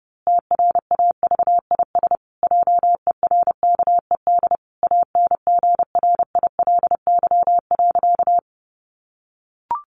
Morse code, heard as a keyed tone, about 30 wpm, 700 Hz. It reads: TRAVIS JERKED ANGRILY.
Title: A Sound Of Thunder 30wpm